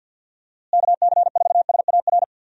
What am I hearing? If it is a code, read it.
Morse code, heard as a keyed tone, about 50 wpm, 700 Hz. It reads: KC4SAR
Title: USA Call Signs - 6 Characters - Repeated 2x - 50wpm